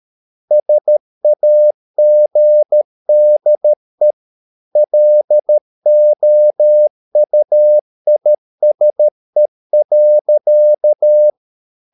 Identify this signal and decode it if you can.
Morse code, heard as a keyed tone, about 13 wpm, 600 Hz. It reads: SAGDE LOUISE.
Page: Ferie 13 wpm